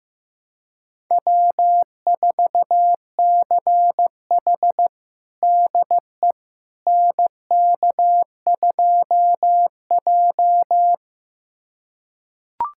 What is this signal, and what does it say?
Morse code, heard as a keyed tone, about 15 wpm, 700 Hz. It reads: W4CH DE NK2J
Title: USA Callsign DE Callsign - 4 Characters - 15wpm